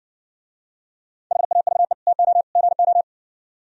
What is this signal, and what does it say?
Morse code, heard as a keyed tone, about 59 wpm, 700 Hz. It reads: HAVE TO GO